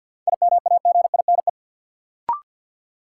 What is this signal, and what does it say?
Morse code, heard as a keyed tone, about 53 wpm, 700 Hz. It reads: IMAGINE
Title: Sentences from Top 1400 Words - New Word Drill - 50wpm